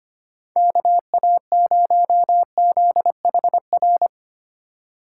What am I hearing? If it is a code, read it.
Morse code, heard as a keyed tone, about 32 wpm, 700 Hz. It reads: KA0ZHR